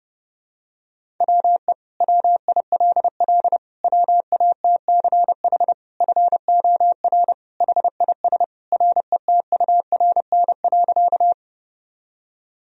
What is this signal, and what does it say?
Morse code, heard as a keyed tone, about 30 wpm, 700 Hz. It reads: WE WILL WATCH FOR HIS RETURN.
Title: Sentences from Top 1000 Words - Review Part 1 - 30wpm